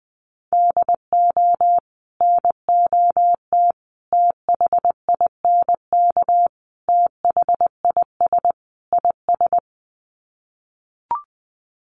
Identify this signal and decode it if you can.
Morse code, heard as a keyed tone, about 20 wpm, 700 Hz. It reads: DO NOT THINK THIS IS
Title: Sets of 5 Words Encore2 20wpm